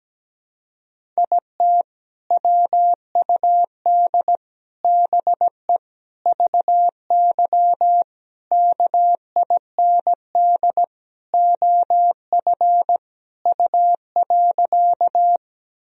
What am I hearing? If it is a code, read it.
Morse code, heard as a keyed tone, about 17 wpm, 700 Hz. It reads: IT WUD BE VY KIND OF U.
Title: Sentences from Top 200 Words and Abbreviations 17wpm